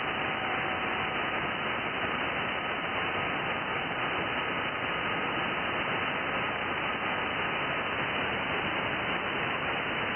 FT8.
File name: IC-7100